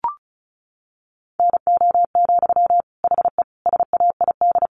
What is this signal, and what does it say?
Morse code, heard as a keyed tone, about 34 wpm, 700 Hz. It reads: NO, HE SAID
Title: Sentences from Top 1100 Words 35wpm